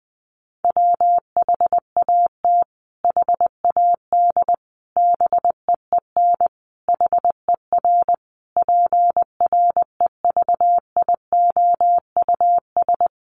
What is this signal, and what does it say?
Morse code, heard as a keyed tone, about 20 wpm, 700 Hz. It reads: WHAT HAD BEEN HER PREVIOUS
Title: Sentences from Top 1500 Words 20wpm